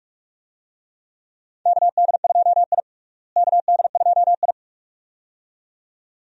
Morse code, heard as a keyed tone, about 46 wpm, 700 Hz. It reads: KD2I KD2I
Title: USA Call Signs - 4 Characters - Repeated 2x - 45wpm